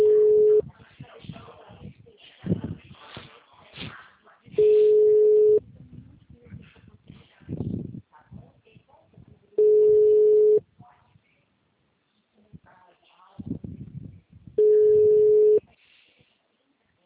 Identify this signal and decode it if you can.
Ringback tone (Europe, 425 Hz), with speech.
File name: out- -2016- - - .18245